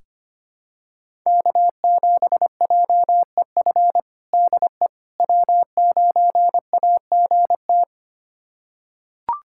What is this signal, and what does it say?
Morse code, heard as a keyed tone, about 25 wpm, 700 Hz. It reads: K7JEF DE W9AGT
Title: USA Callsign DE Callsign - 5 Characters - 25wpm